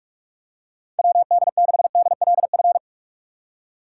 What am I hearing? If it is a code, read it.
Morse code, heard as a keyed tone, about 46 wpm, 700 Hz. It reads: WD6DLF
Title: USA Call Signs - 6 Characters - 4x Character Spacing - 45wpm